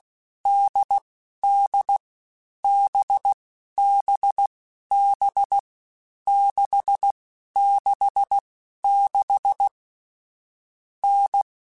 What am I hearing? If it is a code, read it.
Morse code, heard as a keyed tone, about 16 wpm, 800 Hz. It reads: D D B B B 6 6 6 N
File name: Difficult Combos / Difficult Combos-mp3s / NDB NDB6 @15